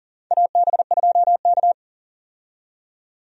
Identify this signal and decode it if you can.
Morse code, heard as a keyed tone, about 40 wpm, 700 Hz. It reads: AB2K
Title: USA Call Signs - 4 Characters - 40wpm